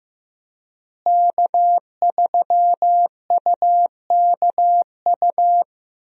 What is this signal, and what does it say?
Morse code, heard as a keyed tone, about 15 wpm, 700 Hz. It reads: K3UKU
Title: USA Call Signs - 5 Characters - Repeated 3x - 15wpm